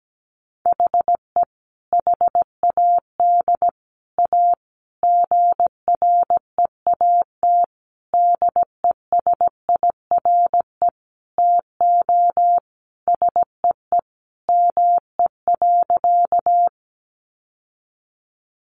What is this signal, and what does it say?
Morse code, heard as a keyed tone, about 17 wpm, 700 Hz. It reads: HE HAD A GREAT DESIRE TO SEE ME.
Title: Sentences from Top 1100 Words 17wpm